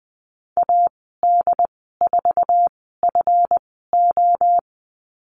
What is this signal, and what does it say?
Morse code, heard as a keyed tone, about 20 wpm, spaced as for 15 wpm, 700 Hz. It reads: AD4FO
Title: 140 Lesson 23g - Letter f - Call Signs 20wpm